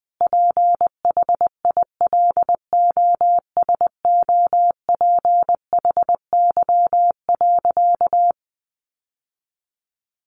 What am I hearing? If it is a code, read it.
Morse code, heard as a keyed tone, about 20 wpm, 700 Hz. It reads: PHILOSOPHY.
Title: Sentences from Top 900 Words 20wpm